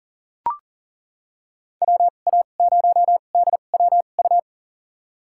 Morse code, heard as a keyed tone, about 40 wpm, spaced as for 29 wpm, 700 Hz. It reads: WA0DWU
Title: USA Call Signs - 6 Characters - 2x Character Spacing - 40wpm